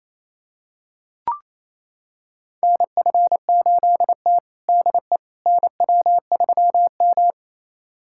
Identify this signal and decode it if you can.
Morse code, heard as a keyed tone, about 28 wpm, 700 Hz. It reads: NF8T DE NW3M
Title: USA Callsign DE Callsign - 4 Characters - 28wpm